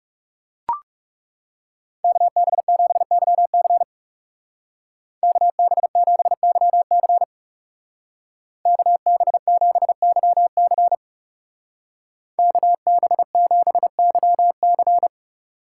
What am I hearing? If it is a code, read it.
Morse code, heard as a keyed tone, about 36 wpm, 700 Hz. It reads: KB7YC KB7YC KB7YC KB7YC
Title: Speed Racing - USA 2x2 Call Signs 30wpm